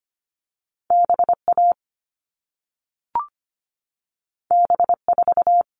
Morse code, heard as a keyed tone, about 25 wpm, 700 Hz. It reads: BA B4
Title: All Cut Numbers - 2-Digit 25wpm